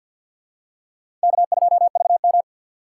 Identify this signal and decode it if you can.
Morse code, heard as a keyed tone, about 51 wpm, 700 Hz. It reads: K2VM
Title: USA Call Signs - 4 Characters - 2x Character Spacing - 50wpm